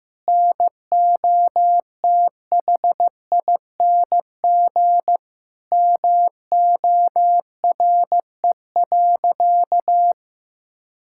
Morse code, heard as a keyed tone, about 16 wpm, 700 Hz. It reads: NOTHING MORE.
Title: Sentences from Top 1000 Words - Review Part 5 - 15wpm